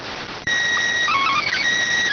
SSTV (slow-scan television), PD 290.